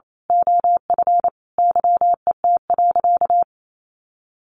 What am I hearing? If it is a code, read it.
Morse code, heard as a keyed tone, about 28 wpm, 700 Hz. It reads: OF YET.
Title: A Sound Of Thunder 28wpm